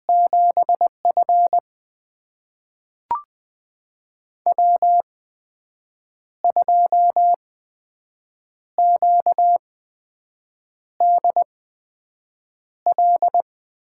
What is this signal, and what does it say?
Morse code, heard as a keyed tone, about 20 wpm, spaced as for 6 wpm, 700 Hz. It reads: 7F W2QDL
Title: USA Call Signs - 8x Character Spacing - 20wpm